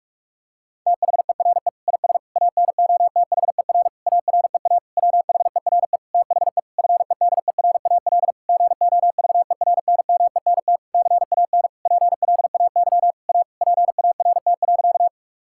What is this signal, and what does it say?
Morse code, heard as a keyed tone, about 41 wpm, 700 Hz. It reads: THERE IS ANOTHER AREA WHERE THE FEDERAL GOVERNMENT CAN PLAY A PART.
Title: Sentences from Top 1000 Words - Review Part 5 - 45wpm